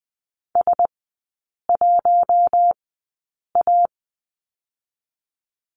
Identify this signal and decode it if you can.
Morse code, heard as a keyed tone, about 20 wpm, 690 Hz. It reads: S 1 A
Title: 145 Lesson 24e - Letter y - Three Characters 20wpm